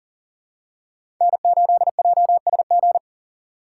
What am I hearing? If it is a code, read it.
Morse code, heard as a keyed tone, about 37 wpm, 700 Hz. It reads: N8JSG
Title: USA Call Signs - 5 Characters - 6x Character Spacing - 40wpm